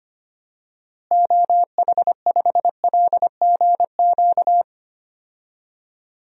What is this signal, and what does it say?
Morse code, heard as a keyed tone, about 25 wpm, 700 Hz. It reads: OH5LGQ
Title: DXCC-100 Call Signs - 6 Characters - Repeated 2x - 25wpm